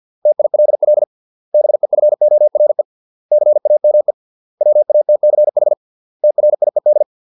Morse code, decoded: TILL BEFORE CAME WATCH TRIED

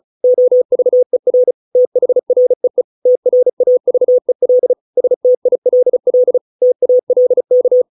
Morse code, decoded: OVER THREE TRAVEL STILL TALK

